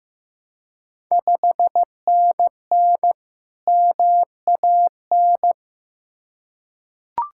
Morse code, heard as a keyed tone, about 15 wpm, 700 Hz. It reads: 5NN MAN
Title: Ontario QSO Party Exchange 15wpm